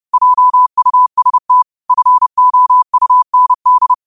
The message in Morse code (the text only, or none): JUST FOUND